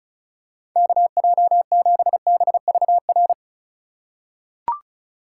Morse code, heard as a keyed tone, about 34 wpm, 700 Hz. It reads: KJ7BVR
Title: USA Call Signs - 6 Characters - 3x Character Spacing - 35wpm